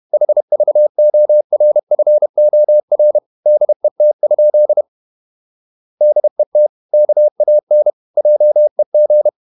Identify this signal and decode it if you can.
Morse code, heard as a keyed tone, about 31 wpm, 600 Hz. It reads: HVORFOR DET? DET KAN JEG
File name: Ferie QRQ 31wpm